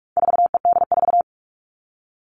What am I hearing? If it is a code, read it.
Morse code, heard as a keyed tone, about 44 wpm, 700 Hz. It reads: 4ED4
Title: All Cut Numbers - 4-Digit 45wpm